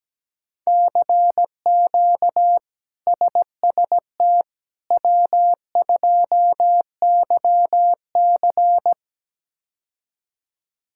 Morse code, heard as a keyed tone, about 17 wpm, 700 Hz. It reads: CQ SST W2YC
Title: K1USN SST QSO Practice - 17wpm